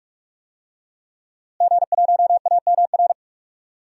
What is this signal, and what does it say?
Morse code, heard as a keyed tone, about 46 wpm, 700 Hz. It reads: G1AMR
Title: DXCC-100 Call Signs - 5 Characters - Repeated 3x - 45wpm